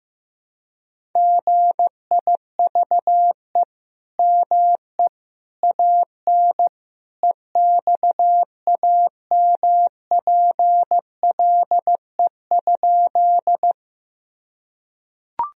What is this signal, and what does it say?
Morse code, heard as a keyed tone, about 16 wpm, 700 Hz. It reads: GIVE ME AN EXAMPLE?
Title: Sentences from Top 1000 Words - Review Part 1 - 15wpm